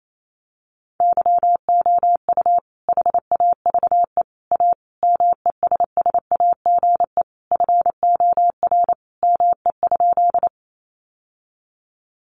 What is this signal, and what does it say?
Morse code, heard as a keyed tone, about 28 wpm, 700 Hz. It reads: YOU HAVE A MESSAGE FOR ME?
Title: Sentences from Top 1100 Words 28wpm